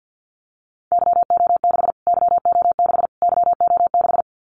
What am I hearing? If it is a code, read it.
Morse code, heard as a keyed tone, about 50 wpm, 700 Hz. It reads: YO6 YO6 YO6
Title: Callsign Prefixes for Top 25 DXCC - Repeated 3x - 50wpm